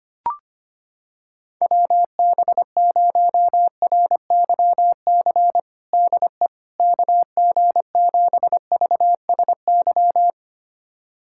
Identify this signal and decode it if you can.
Morse code, heard as a keyed tone, about 25 wpm, 700 Hz. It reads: WB0RYC DE KG7VSY
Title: USA Callsign DE Callsign 25wpm